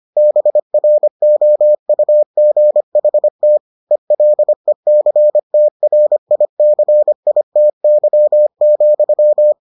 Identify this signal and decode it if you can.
Morse code, heard as a keyed tone, about 25 wpm, 600 Hz. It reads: BROUGHT ELECTRICITY,